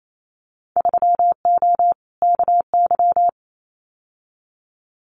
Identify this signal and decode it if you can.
Morse code, heard as a keyed tone, about 28 wpm, 700 Hz. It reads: 3O KY
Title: Winter Field Day Exchanges 28wpm